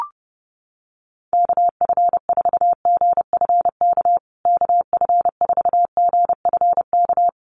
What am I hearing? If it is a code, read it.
Morse code, heard as a keyed tone, about 30 wpm, 700 Hz. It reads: KF4GFK KF4GFK
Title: USA Call Signs Repeated 2x 30wpm